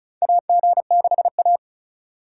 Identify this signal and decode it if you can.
Morse code, heard as a keyed tone, about 36 wpm, 700 Hz. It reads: AG6A
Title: USA Call Signs - 4 Characters - 2x Character Spacing - 35wpm